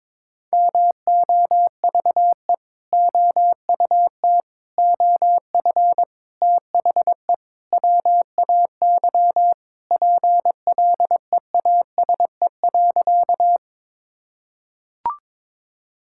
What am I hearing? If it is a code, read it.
Morse code, heard as a keyed tone, about 22 wpm, 700 Hz. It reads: MOVE OUT OF THE WAY PLEASE.
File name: practice / 22wpm / Sentences with 4-Word Phrases - 22wpm